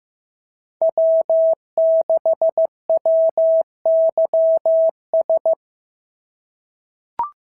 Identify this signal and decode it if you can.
Morse code, heard as a keyed tone, about 14 wpm, 650 Hz. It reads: W6WYS
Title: SOTA QSO Practice 15wpm